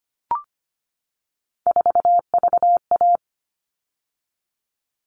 Morse code, heard as a keyed tone, about 25 wpm, 700 Hz. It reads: 4VA
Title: All Cut Numbers - 3-Digit 25wpm